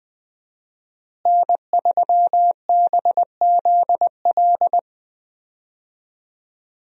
Morse code, heard as a keyed tone, about 20 wpm, 700 Hz. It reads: N3BZL
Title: USA Call Signs - 5 Characters - Repeated 3x - 20wpm